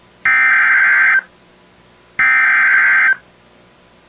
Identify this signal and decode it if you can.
Emergency Alert System (SAME) tones (header bursts), heard as harsh buzzing bursts. It is heard from 0.3 to 3.1 s.